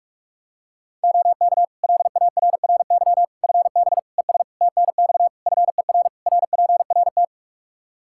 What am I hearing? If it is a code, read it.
Morse code, heard as a keyed tone, about 46 wpm, 700 Hz. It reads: OK LARRY FB ES TNX FER RPRT
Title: Practice QSO - Exchange 2 - Info - 45wpm